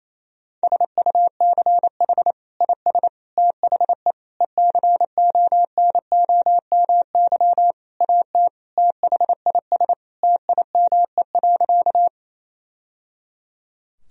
Morse code, heard as a keyed tone, about 28 wpm, 700 Hz. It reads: SUCH IS THE ECONOMY AT THIS TIME.
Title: Sentences from Top 1000 Words - Review Part 3 - 28wpm